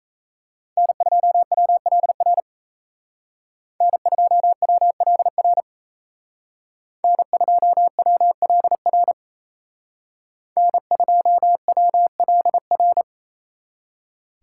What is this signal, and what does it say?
Morse code, heard as a keyed tone, about 33 wpm, 700 Hz. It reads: N2WLR N2WLR N2WLR N2WLR
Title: Speed Racing - USA 1x3 Call Signs 28wpm